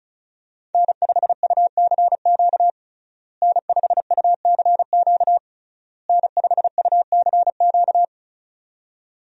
Morse code, heard as a keyed tone, about 36 wpm, 700 Hz. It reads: N5UCQ N5UCQ N5UCQ
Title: USA Call Signs - 5 Characters - Repeated 3x - 35wpm